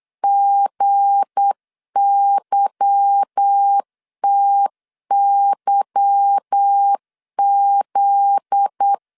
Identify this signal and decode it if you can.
Morse code, heard as a keyed tone, about 8 wpm, 800 Hz. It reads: GYTYZ
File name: LEZIONI / 8 YJ45 / 8 Y 8wpm 3245